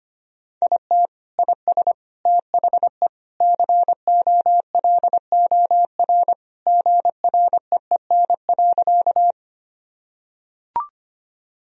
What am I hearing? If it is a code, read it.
Morse code, heard as a keyed tone, about 25 wpm, 700 Hz. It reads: IT IS THE COLOR GREEN.
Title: Sentences from Top 1000 Words - Review Part 2 - 25wpm